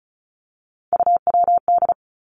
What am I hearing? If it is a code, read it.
Morse code, heard as a keyed tone, about 36 wpm, 700 Hz. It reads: UWD
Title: Three Challenging Letter-Number Combinations 35wpm